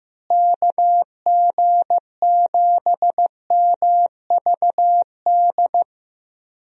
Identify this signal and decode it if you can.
Morse code, heard as a keyed tone, about 15 wpm, 700 Hz. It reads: KG7MVD
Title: USA Call Signs - 6 Characters - Repeated 2x - 15wpm